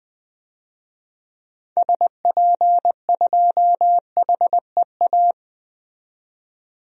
Morse code, heard as a keyed tone, about 20 wpm, 700 Hz. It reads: SP2HEA
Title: DXCC-100 Call Signs - 6 Characters - 20wpm